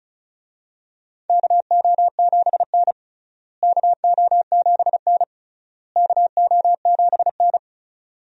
Morse code, heard as a keyed tone, about 36 wpm, 700 Hz. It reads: KO7N KO7N KO7N
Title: USA Call Signs - 4 Characters - Repeated 3x - 35wpm